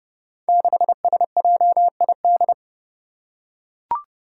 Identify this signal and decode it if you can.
Morse code, heard as a keyed tone, about 29 wpm, 700 Hz. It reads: 6SJID
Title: Five Challenging Letter-Number Combinations 30wpm